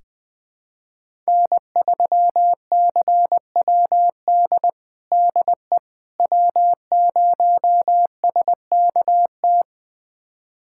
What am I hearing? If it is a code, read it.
Morse code, heard as a keyed tone, about 21 wpm, 700 Hz. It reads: N3CWD DE W0SKT
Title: USA Callsign DE Callsign - 5 Characters - 20wpm